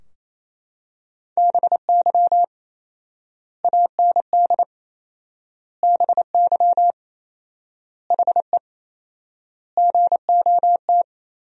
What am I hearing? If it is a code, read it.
Morse code, heard as a keyed tone, about 28 wpm, 700 Hz. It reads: BY AND BY HE GOT
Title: 4x Word-Spacing - Sentences from Top 300 Words - 28wpm